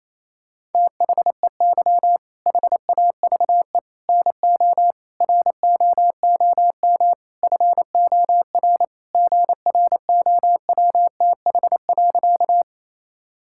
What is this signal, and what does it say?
Morse code, heard as a keyed tone, about 28 wpm, 700 Hz. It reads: THEY HAVE NO ROOM FOR GROWTH.
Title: Sentences from Top 1000 Words - Review Part 5 - 28wpm